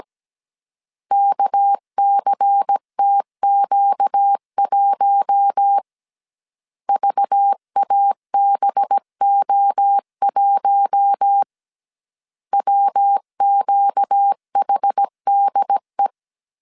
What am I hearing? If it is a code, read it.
Morse code, heard as a keyed tone, about 17 wpm, 800 Hz. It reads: KCTQ1 VABO1 WQHDE
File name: CW / DET / 1 6 1500C S B 16wpm 2619